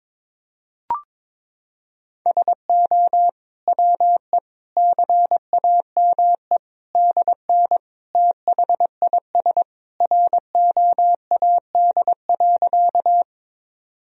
Morse code, heard as a keyed tone, about 22 wpm, 700 Hz. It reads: SO WE CAME DN THIS ROAD.
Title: Sentences from Top 400 Words and Abbreviations 22wpm